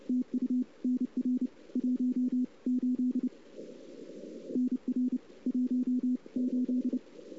Morse code, heard as a keyed tone, about 29 wpm, 250 Hz. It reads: TU NR 1 8 NR 1 8
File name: CW / 2008-ss-cw-k5tr-op-n5rz -03:00